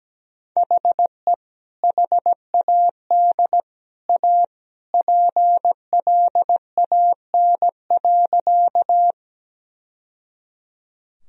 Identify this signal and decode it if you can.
Morse code, heard as a keyed tone, about 17 wpm, 700 Hz. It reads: HE HAD A PLAN.
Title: Sentences from Top 1000 Words - Review Part 2 - 17wpm